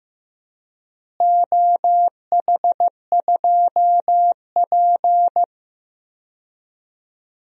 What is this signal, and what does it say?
Morse code, heard as a keyed tone, about 16 wpm, 700 Hz. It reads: OH2P
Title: DXCC-100 Call Signs - 4 Characters - Repeated 3x - 15wpm